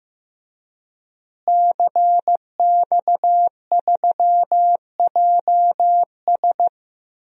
Morse code, heard as a keyed tone, about 15 wpm, 700 Hz. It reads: CX3JS
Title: DXCC-100 Call Signs - 5 Characters - Repeated 2x - 15wpm